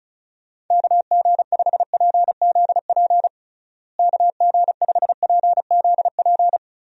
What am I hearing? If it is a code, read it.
Morse code, heard as a keyed tone, about 34 wpm, 700 Hz. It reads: KG5PZP KG5PZP
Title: USA Call Signs - 6 Characters - Repeated 3x - 35wpm